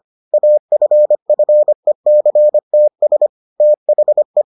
Morse code, heard as a keyed tone, about 25 wpm, 600 Hz. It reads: AFFECTS THE